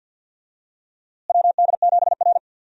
Morse code, heard as a keyed tone, about 48 wpm, 700 Hz. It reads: WD7R
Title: USA Call Signs - 4 Characters - Repeated 3x - 50wpm